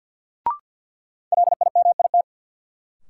Morse code, heard as a keyed tone, about 51 wpm, 700 Hz. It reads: LIMIT
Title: Sentences from Top 1500 Words - New Word Drill Alt - 50wpm